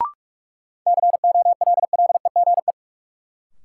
Morse code, heard as a keyed tone, about 41 wpm, 700 Hz. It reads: COLLEGE
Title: Sentences from Top 1300 Words - New Word Drill Alt - 45wpm